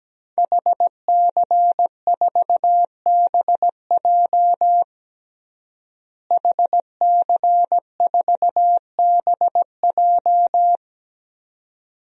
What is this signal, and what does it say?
Morse code, heard as a keyed tone, about 17 wpm, 700 Hz. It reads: HC4BJ HC4BJ
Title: DXCC-100 Call Signs - 5 Characters - Repeated 2x - 17wpm